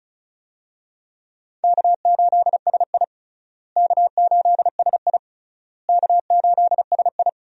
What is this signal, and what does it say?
Morse code, heard as a keyed tone, about 36 wpm, 700 Hz. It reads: K8SI K8SI K8SI
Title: USA Call Signs - 4 Characters - Repeated 3x - 35wpm